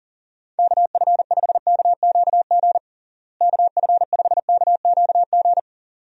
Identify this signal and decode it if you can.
Morse code, heard as a keyed tone, about 40 wpm, 700 Hz. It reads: KF5KQG KF5KQG
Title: USA Call Signs - 6 Characters - Repeated 2x - 40wpm